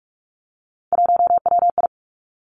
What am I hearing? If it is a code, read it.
Morse code, heard as a keyed tone, about 46 wpm, 700 Hz. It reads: 1WI
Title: Three Challenging Letter-Number Combinations 45wpm